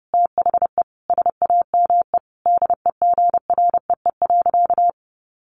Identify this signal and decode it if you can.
Morse code, heard as a keyed tone, about 30 wpm, 700 Hz. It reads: THE SAME DEGREE.
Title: Sentences from Top 1500 Words 30wpm